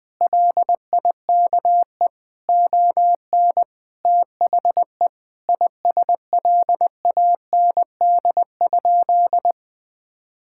Morse code, heard as a keyed tone, about 21 wpm, 700 Hz. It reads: LIKE ON THE ISLAND?
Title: Sentences from Top 1600 Words 20wpm